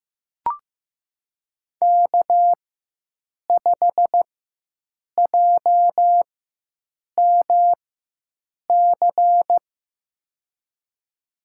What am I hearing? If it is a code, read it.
Morse code, heard as a keyed tone, about 16 wpm, spaced as for 7 wpm, 700 Hz. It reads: K5JMC
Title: USA Call Signs - 5 Characters - 4x Character Spacing - 15wpm